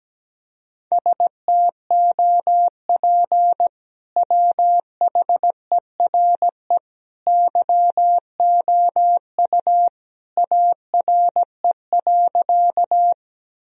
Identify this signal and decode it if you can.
Morse code, heard as a keyed tone, about 17 wpm, 700 Hz. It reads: STOP WHERE YOU ARE.
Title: Sentences from Top 1000 Words - Review Part 1 - 17wpm